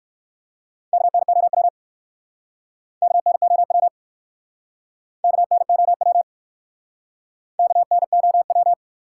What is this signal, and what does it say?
Morse code, heard as a keyed tone, about 54 wpm, 700 Hz. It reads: KNOW KNOW KNOW KNOW
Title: Speed-Racing Top 180 QSO Elements 45wpm